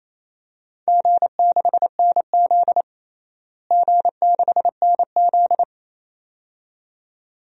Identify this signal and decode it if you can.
Morse code, heard as a keyed tone, about 27 wpm, 700 Hz. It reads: G6NZ G6NZ
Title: DXCC-100 Call Signs - 4 Characters - Repeated 2x - 28wpm